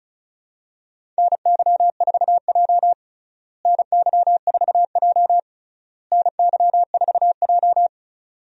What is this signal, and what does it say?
Morse code, heard as a keyed tone, about 36 wpm, 700 Hz. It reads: NY4J NY4J NY4J
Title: USA Call Signs - 4 Characters - Repeated 3x - 35wpm